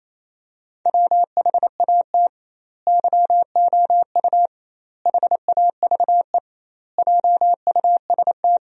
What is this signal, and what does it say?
Morse code, heard as a keyed tone, about 28 wpm, 700 Hz. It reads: WHAT YOU HAVE JUST
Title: 2x Word-Spacing - Sentences from Top 200 Words - 28wpm